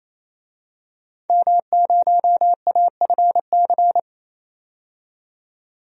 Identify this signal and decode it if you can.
Morse code, heard as a keyed tone, about 28 wpm, 700 Hz. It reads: M0AFC
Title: DXCC-100 Call Signs - 5 Characters - 28wpm